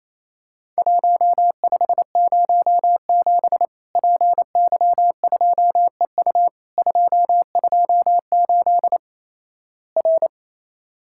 Morse code, heard as a keyed tone, about 28 wpm, 700 Hz. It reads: 1507 PY2EU 228 R
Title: Worked All Europe DX Contest - QTC Exchange - 28wpm